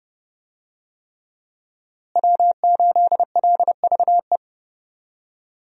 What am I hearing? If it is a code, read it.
Morse code, heard as a keyed tone, about 30 wpm, 700 Hz. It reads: W8LVE
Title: USA Call Signs - 5 Characters - Repeated 3x - 30wpm